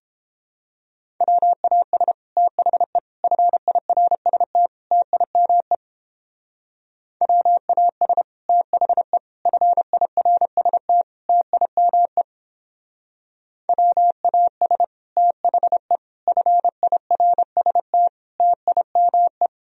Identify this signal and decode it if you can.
Morse code, heard as a keyed tone, about 31 wpm, 700 Hz. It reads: WAS THE FIRST TIME WAS THE FIRST TIME WAS THE FIRST TIME